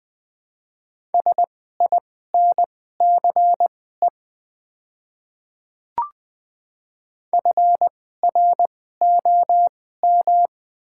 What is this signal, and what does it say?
Morse code, heard as a keyed tone, about 20 wpm, spaced as for 14 wpm, 700 Hz. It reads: SINCE FROM
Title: 2x Character Spacing - Top 500 Words - 20wpm